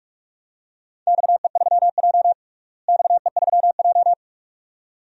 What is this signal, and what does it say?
Morse code, heard as a keyed tone, about 41 wpm, 700 Hz. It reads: XE3J XE3J
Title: DXCC-100 Call Signs - 4 Characters - Repeated 2x - 45wpm